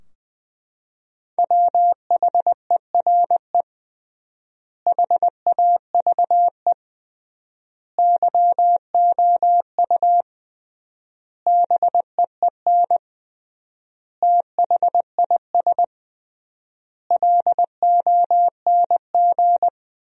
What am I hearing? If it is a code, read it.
Morse code, heard as a keyed tone, about 21 wpm, 700 Hz. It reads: WHERE HAVE YOU BEEN THIS LONG
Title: 3x Word-Spacing - Sentences from Top 300 Words - 20wpm